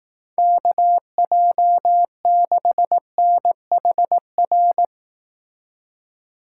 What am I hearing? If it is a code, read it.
Morse code, heard as a keyed tone, about 18 wpm, 700 Hz. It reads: KJ6NHR
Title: Speed Racing - USA 2x3 Call Signs 15wpm